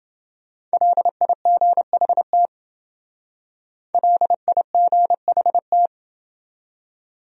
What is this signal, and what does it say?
Morse code, heard as a keyed tone, about 29 wpm, 700 Hz. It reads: LIGHT LIGHT